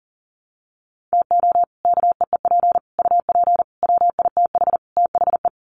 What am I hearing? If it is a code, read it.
Morse code, heard as a keyed tone, about 40 wpm, 700 Hz. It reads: TO KEEP UP WITH THE
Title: Sets of 5 Words Encore 40wpm